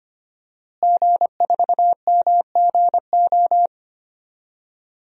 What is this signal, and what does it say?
Morse code, heard as a keyed tone, about 25 wpm, 700 Hz. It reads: G4MGO
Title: DXCC-100 Call Signs - 5 Characters - Repeated 2x - 25wpm